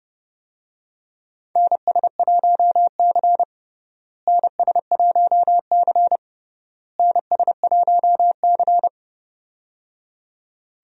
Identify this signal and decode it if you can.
Morse code, heard as a keyed tone, about 30 wpm, 700 Hz. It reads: NS1C NS1C NS1C
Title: USA Call Signs - 4 Characters - Repeated 3x - 30wpm